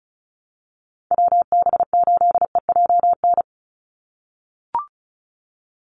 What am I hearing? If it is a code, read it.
Morse code, heard as a keyed tone, about 34 wpm, 700 Hz. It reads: WB8EJN
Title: Call Signs - 35wpm